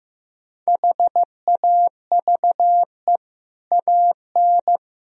Morse code, heard as a keyed tone, about 16 wpm, 700 Hz. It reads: HAVE AN
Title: Sentences from Top 1000 Words - Review Part 1 - 15wpm